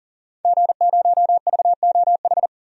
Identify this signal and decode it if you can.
Morse code, heard as a keyed tone, about 40 wpm, 700 Hz. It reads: G0VOH
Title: DXCC-100 Call Signs - 5 Characters - 40wpm